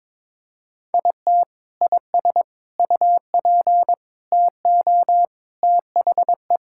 Morse code, heard as a keyed tone, about 21 wpm, 700 Hz. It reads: IT IS UP TO THE
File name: practice / 22wpm / Sentences with 3-Word Phrases - 22wpm